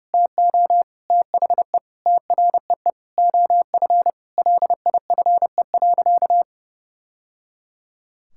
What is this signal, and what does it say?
Morse code, heard as a keyed tone, about 28 wpm, 700 Hz. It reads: TO THE TREE OF LIFE.
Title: Sentences from Top 1000 Words - Review Part 6 - 30wpm